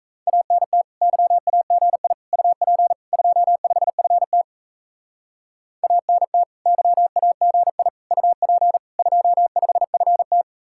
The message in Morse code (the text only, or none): ANT YAGI UP 25FT ANT YAGI UP 25FT